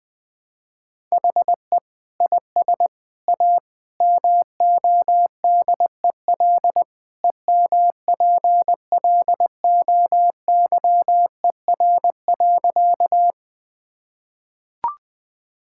Morse code, heard as a keyed tone, about 20 wpm, 700 Hz. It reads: HE IS A MODEL EMPLOYER.
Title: Sentences from Top 1600 Words 20wpm